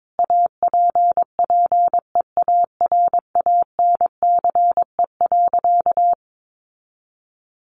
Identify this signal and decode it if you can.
Morse code, heard as a keyed tone, about 22 wpm, 700 Hz. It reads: APPEARANCE.
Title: Sentences from Top 1100 Words 22wpm